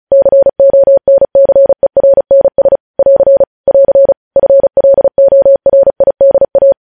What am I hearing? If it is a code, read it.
Morse code, heard as a keyed tone, about 36 wpm, 550 Hz. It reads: CONCERNS <AR> <AR> FLORIDA
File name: podcast / News headlines-1x-35x35